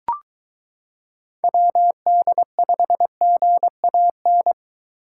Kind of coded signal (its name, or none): Morse code